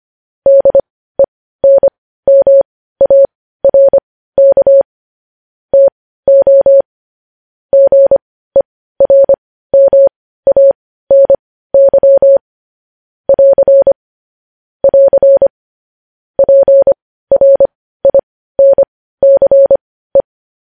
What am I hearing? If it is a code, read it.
Morse code, heard as a keyed tone, about 25 wpm, spaced as for 15 wpm, 550 Hz. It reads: DENMARK TO GERMANY <AR> <AR> PRINCE
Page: News Headlines in Morse Code at 15 WPM